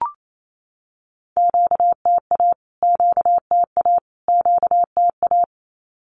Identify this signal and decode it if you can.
Morse code, heard as a keyed tone, about 28 wpm, 700 Hz. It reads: QTA QTA QTA
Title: Q-Codes Repeated 3x 28wpm